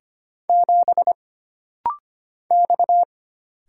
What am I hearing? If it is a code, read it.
Morse code, heard as a keyed tone, about 25 wpm, 700 Hz. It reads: 7 X
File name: Single Character (Letter:Number:Punctuation:Prosign) - Rapid-Fire - 25wpm